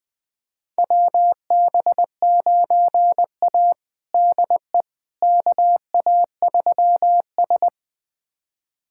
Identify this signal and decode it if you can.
Morse code, heard as a keyed tone, about 20 wpm, 700 Hz. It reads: WB9A DE KA3S
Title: USA Callsign DE Callsign - 4 Characters - 20wpm